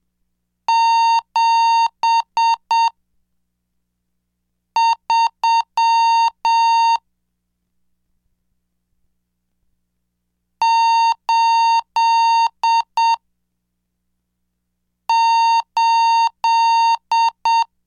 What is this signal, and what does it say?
Morse code, heard as a keyed tone, about 7 wpm, spaced as for 3 wpm, 920 Hz. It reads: 73 88
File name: cursos / CDestudo / 24 Track 24